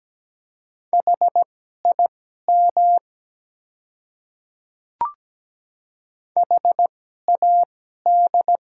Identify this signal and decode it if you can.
Morse code, heard as a keyed tone, about 17 wpm, spaced as for 12 wpm, 700 Hz. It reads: HIM HAD
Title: 2x Character Spacing - Top 100 Words - 17wpm